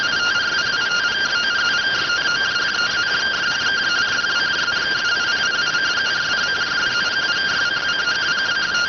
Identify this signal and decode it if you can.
Two-tone teleprinter signal, about 170 Hz shift.